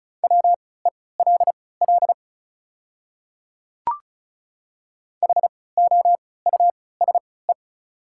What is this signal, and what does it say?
Morse code, heard as a keyed tone, about 34 wpm, spaced as for 20 wpm, 690 Hz. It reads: WELL HOUSE
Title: 3x Character Spacing - Top 300 Words - 35wpm